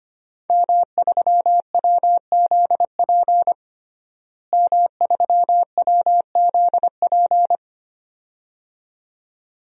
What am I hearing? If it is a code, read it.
Morse code, heard as a keyed tone, about 25 wpm, 700 Hz. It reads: M3WZP M3WZP
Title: DXCC-100 Call Signs - 5 Characters - Repeated 2x - 25wpm